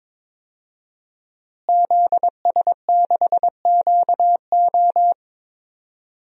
Morse code, heard as a keyed tone, about 22 wpm, 700 Hz. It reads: ZS6QO
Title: DXCC-100 Call Signs - 5 Characters - Repeated 3x - 22wpm